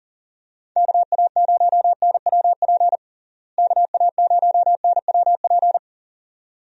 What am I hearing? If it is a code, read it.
Morse code, heard as a keyed tone, about 40 wpm, 700 Hz. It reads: KA0NWP KA0NWP
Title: USA Call Signs - 6 Characters - Repeated 2x - 40wpm